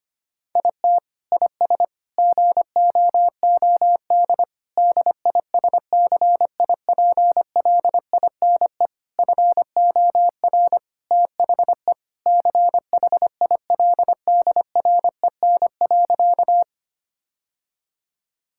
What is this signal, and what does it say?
Morse code, heard as a keyed tone, about 25 wpm, 700 Hz. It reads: IT IS GOOD DISCIPLINE FOR THE CHILDREN.
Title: Sentences from Top 1000 Words - Review Part 6 - 25wpm